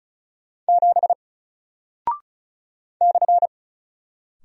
Morse code, heard as a keyed tone, about 36 wpm, 700 Hz. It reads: 7 /
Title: Single Character (Letter:Number:Punctuation:Prosign) Alt 35wpm